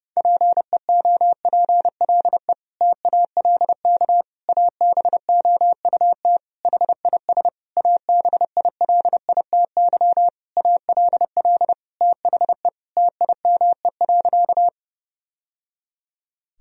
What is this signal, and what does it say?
Morse code, heard as a keyed tone, about 30 wpm, 700 Hz. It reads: PEOPLE TALK ABOUT HIS ABILITY ALL THE TIME.
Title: Sentences from Top 1000 Words - Review Part 3 - 30wpm